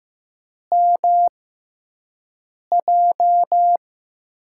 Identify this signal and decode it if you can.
Morse code, heard as a keyed tone, about 15 wpm, spaced as for 5 wpm, 700 Hz. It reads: MJ
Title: USA Call Signs - 5 Characters - 6x Character Spacing - 15wpm